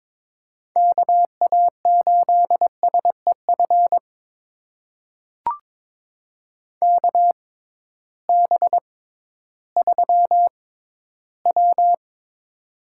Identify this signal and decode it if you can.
Morse code, heard as a keyed tone, about 22 wpm, spaced as for 13 wpm, 700 Hz. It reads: KA8SEF KB3W
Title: USA Call Signs - 6 Characters - 6x Character Spacing - 22wpm